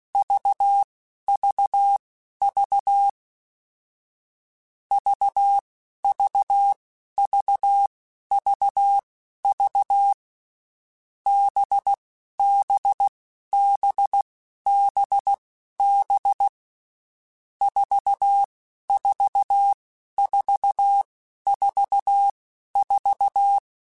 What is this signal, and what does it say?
Morse code, heard as a keyed tone, about 16 wpm, spaced as for 11 wpm, 800 Hz. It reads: VVV VVVVV BBBBB 44444
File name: Groups@10